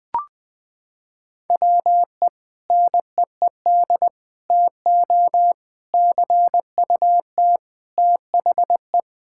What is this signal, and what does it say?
Morse code, heard as a keyed tone, about 20 wpm, 700 Hz. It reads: WE NEED TO CUT THE
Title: Sentences from Top 1000 Words - Review Part 6 - 20wpm